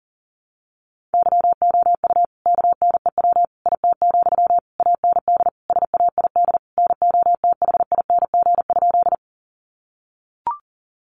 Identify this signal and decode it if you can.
Morse code, heard as a keyed tone, about 40 wpm, 700 Hz. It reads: YOU KNEW IT, AND SAID NOTHING?
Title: Sentences from Top 500 Words 40wpm